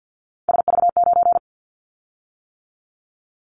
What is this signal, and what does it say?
Morse code, heard as a keyed tone, about 48 wpm, 700 Hz. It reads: SV9
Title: Callsign Prefixes for Top 100 DXCC - 50wpm